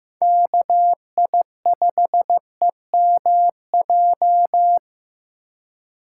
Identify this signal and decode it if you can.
Morse code, heard as a keyed tone, about 15 wpm, 700 Hz. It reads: KI5EMJ
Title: USA Call Signs - 6 Characters - 3x Character Spacing - 15wpm